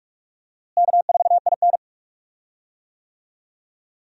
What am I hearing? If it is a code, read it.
Morse code, heard as a keyed tone, about 46 wpm, 700 Hz. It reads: K4IN
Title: USA Call Signs - 4 Characters - Repeated 3x - 45wpm